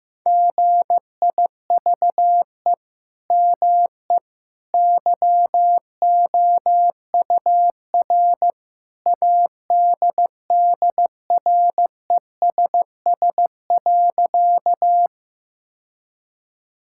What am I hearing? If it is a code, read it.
Morse code, heard as a keyed tone, about 15 wpm, 700 Hz. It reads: GIVE ME YOUR ADDRESS.
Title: Sentences from Top 1000 Words - Review Part 6 - 15wpm